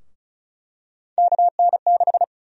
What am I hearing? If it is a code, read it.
Morse code, heard as a keyed tone, about 36 wpm, 700 Hz. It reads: KN6
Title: USA Call Sign Prefixes - Repeated 2x - 35wpm